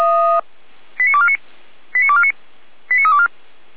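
Five-tone selective call (ZVEI), heard as quick runs of five tones, tones of about 70 ms. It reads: 89249 89249 89324
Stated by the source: ZVEI five-tone selective call received by SDR, from the Feuerwehr Orlamünde fire brigade server.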